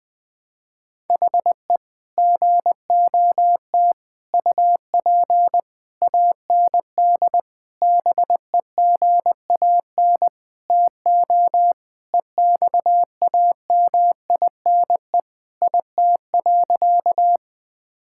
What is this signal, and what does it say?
Morse code, heard as a keyed tone, about 20 wpm, 700 Hz. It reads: HE GOT UP AND BEGAN TO EXAMINE IT.
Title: Sentences from Top 1600 Words 20wpm